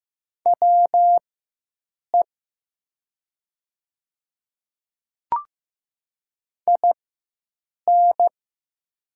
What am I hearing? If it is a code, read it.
Morse code, heard as a keyed tone, about 15 wpm, spaced as for 7 wpm, 700 Hz. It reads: WE IN
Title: 4x Character Spacing - Top 100 Words - 15wpm